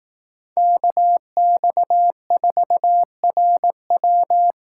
Morse code, heard as a keyed tone, about 18 wpm, 700 Hz. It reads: KX4RW
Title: Speed Racing - USA 2x2 Call Signs 15wpm